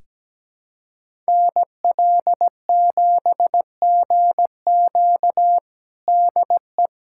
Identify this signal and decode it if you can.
Morse code, heard as a keyed tone, about 17 wpm, 700 Hz. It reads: NL7GQ DE
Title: USA Callsign DE Callsign - 5 Characters - 17wpm